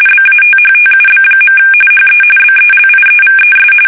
RTTY, about 45.45 baud. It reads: BATHTUB. HE BELIEVES
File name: fsk850 / battleshipnj